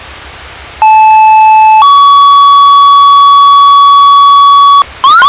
Two-tone page, heard as a one-second tone, then a three-second tone, about 870 then 1130 Hz, with noise. It is heard from 0.8 to 4.8 s.